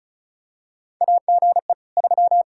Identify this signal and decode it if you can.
Morse code, heard as a keyed tone, about 34 wpm, 700 Hz. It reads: AGE 3
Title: Practice QSO - Exchange 3 - More Info - 35wpm